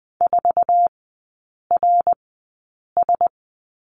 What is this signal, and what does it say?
Morse code, heard as a keyed tone, about 20 wpm, spaced as for 8 wpm, 700 Hz. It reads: 4RS
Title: 053 Lesson 10e - Letter r - Three Characters 20wpm